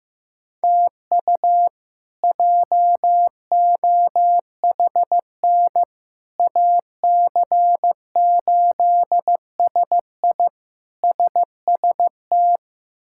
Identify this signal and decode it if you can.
Morse code, heard as a keyed tone, about 16 wpm, 700 Hz. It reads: TU JOHN AC8SI SST
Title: K1USN SST QSO Practice - 15wpm